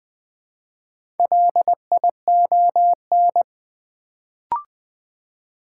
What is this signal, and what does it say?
Morse code, heard as a keyed tone, about 20 wpm, 700 Hz. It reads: LION
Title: Build A Word - 1 Letter At A Time - 3 to 5-Letter Words - 20wpm